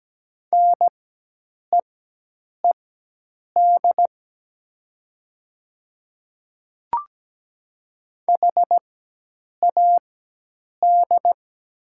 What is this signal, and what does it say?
Morse code, heard as a keyed tone, about 17 wpm, spaced as for 8 wpm, 700 Hz. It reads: NEED HAD
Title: 4x Character Spacing - Top 200 Words - 17wpm